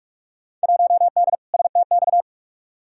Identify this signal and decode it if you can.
Morse code, heard as a keyed tone, about 44 wpm, 700 Hz. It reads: 1D STX
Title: Field Day Exchanges Encore - 45wpm